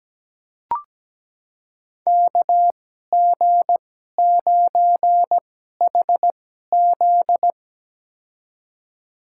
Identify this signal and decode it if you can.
Morse code, heard as a keyed tone, about 17 wpm, spaced as for 12 wpm, 700 Hz. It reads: KG9HZ
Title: USA Call Signs - 2x Character Spacing - 17wpm